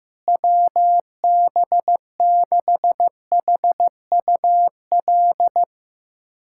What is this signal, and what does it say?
Morse code, heard as a keyed tone, about 15 wpm, 700 Hz. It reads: WB6HUL
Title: USA Callsign DE Callsign 15wpm